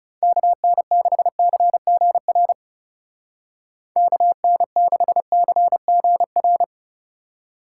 Morse code, heard as a keyed tone, about 35 wpm, 700 Hz. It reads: KN6CGR KN6CGR
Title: Speed Racing - USA 2x3 Call Signs 30wpm